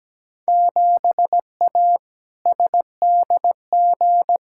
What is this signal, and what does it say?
Morse code, heard as a keyed tone, about 17 wpm, 700 Hz. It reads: 7A SDG
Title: Field Day Exchanges Encore - 17wpm